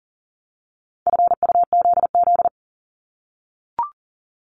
Morse code, heard as a keyed tone, about 38 wpm, 700 Hz. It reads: FUZZ
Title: Multi-letter Phonemes 40wpm